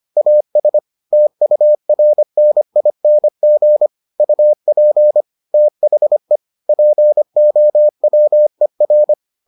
Morse code, heard as a keyed tone, about 25 wpm, 600 Hz. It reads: AS TURNING UP THE POWER